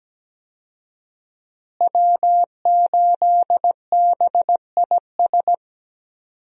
Morse code, heard as a keyed tone, about 17 wpm, 700 Hz. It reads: W8BIS
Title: USA Call Signs - 5 Characters - Repeated 3x - 17wpm